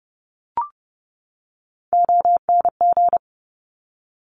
Morse code, heard as a keyed tone, about 30 wpm, 700 Hz. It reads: ONG
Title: Top Three Letter Combinations in Words 30wpm